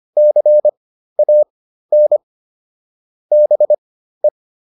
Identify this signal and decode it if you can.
Morse code, heard as a keyed tone, about 25 wpm, spaced as for 13 wpm, 600 Hz. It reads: CAN BE